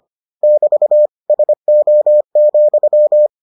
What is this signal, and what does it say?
Morse code, heard as a keyed tone, about 25 wpm, 600 Hz. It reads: <BT> SO,